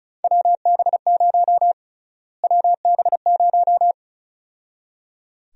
Morse code, heard as a keyed tone, about 36 wpm, 700 Hz. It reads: WB0 WB0
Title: USA Call Sign Prefixes - Repeated 2x - 35wpm